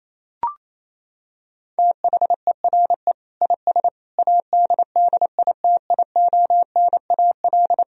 Morse code, heard as a keyed tone, about 28 wpm, 700 Hz. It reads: THERE IS ADDITIONAL
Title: Sentences from Top 1000 Words - Review Part 4 - 28wpm